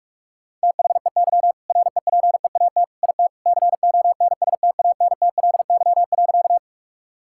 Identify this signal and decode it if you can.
Morse code, heard as a keyed tone, about 46 wpm, 700 Hz. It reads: THEY REPEAT IT CONSTANTLY.
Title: Sentences from Top 1600 Words 45wpm